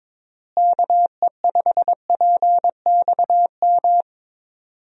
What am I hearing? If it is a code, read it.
Morse code, heard as a keyed tone, about 22 wpm, 700 Hz. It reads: KE5PXM
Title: USA Call Signs - 6 Characters - Repeated 3x - 22wpm